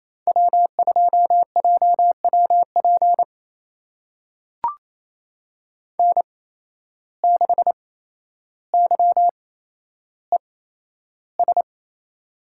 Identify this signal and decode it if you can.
Morse code, heard as a keyed tone, about 28 wpm, spaced as for 12 wpm, 700 Hz. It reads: W2JWP N6YES
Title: USA Call Signs - 5 Characters - 8x Character Spacing - 28wpm